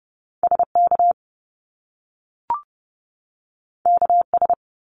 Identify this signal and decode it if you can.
Morse code, heard as a keyed tone, about 30 wpm, 700 Hz. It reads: SK KS
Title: Top Two Letter Combinations in Words 30wpm